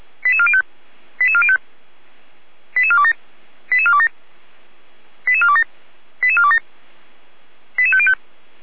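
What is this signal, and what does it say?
Five-tone selective call (ZVEI), heard as quick runs of five tones, tones of about 70 ms. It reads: 88405 88405 88427 88427 88427 88427 88505
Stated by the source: ZVEI five-tone selective call received by SDR.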